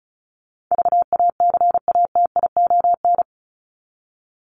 Morse code, heard as a keyed tone, about 34 wpm, 700 Hz. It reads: VACATION
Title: Top 1501-1750 Words 35wpm